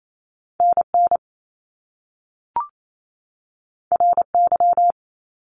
Morse code, heard as a keyed tone, about 28 wpm, 700 Hz. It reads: NN RY
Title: Top Two Letter Combinations in Words 28wpm